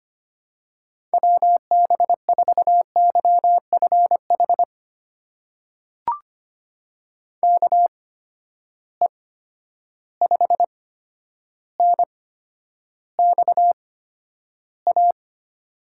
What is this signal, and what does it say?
Morse code, heard as a keyed tone, about 25 wpm, spaced as for 11 wpm, 700 Hz. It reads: WB4YFH KE5NXA
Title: USA Call Signs - 6 Characters - 8x Character Spacing - 25wpm